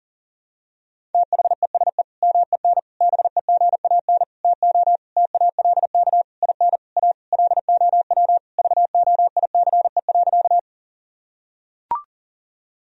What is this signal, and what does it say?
Morse code, heard as a keyed tone, about 40 wpm, 700 Hz. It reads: THESE MEN BEGAN TO TALK IN A LOW VOICE.
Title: Sentences from Top 1000 Words - Review Part 1 - 40wpm